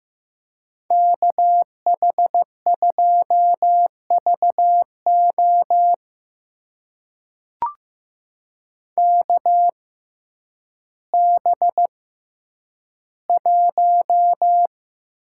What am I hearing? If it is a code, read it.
Morse code, heard as a keyed tone, about 15 wpm, spaced as for 9 wpm, 700 Hz. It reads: KH2VO KB1
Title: USA Call Signs - 6x Character Spacing - 15wpm